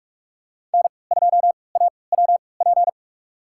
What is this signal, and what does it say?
Morse code, heard as a keyed tone, about 46 wpm, spaced as for 25 wpm, 700 Hz. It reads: N2AWP
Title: USA Call Signs - 5 Characters - 3x Character Spacing - 45wpm